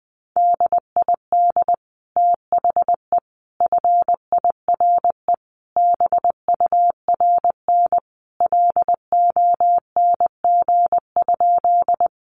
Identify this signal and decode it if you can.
Morse code, heard as a keyed tone, about 20 wpm, 700 Hz. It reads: DID THE FIRE BURN LONG?
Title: Sentences from Top 1300 Words 20wpm